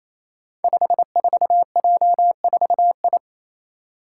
Morse code, heard as a keyed tone, about 27 wpm, 700 Hz. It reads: 54J4I
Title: Five Challenging Letter-Number Combinations 28wpm